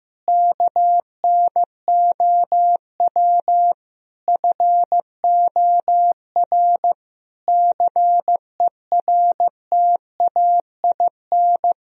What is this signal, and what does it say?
Morse code, heard as a keyed tone, about 15 wpm, 700 Hz. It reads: KNOW FOR CERTAIN
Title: Sentences from Top 1000 Words - Review Part 2 - 15wpm